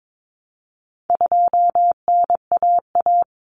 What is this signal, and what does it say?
Morse code, heard as a keyed tone, about 22 wpm, 700 Hz. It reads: 2NAA
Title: Common Cut Numbers - 4-Digit 22wpm